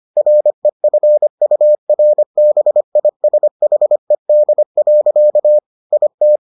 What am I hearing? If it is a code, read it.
Morse code, heard as a keyed tone, about 25 wpm, 600 Hz. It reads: REFURBISHED. IT